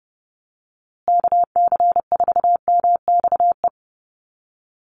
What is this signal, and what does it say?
Morse code, heard as a keyed tone, about 30 wpm, 700 Hz. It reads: KC4MXE
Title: USA Call Signs Repeated 3x 30wpm